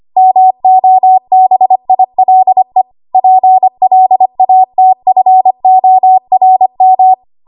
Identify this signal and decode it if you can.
Morse code, heard as a keyed tone, about 25 wpm, 750 Hz. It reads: MOBILE PLATFORM
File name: Archive / 25 WPM / 25WPM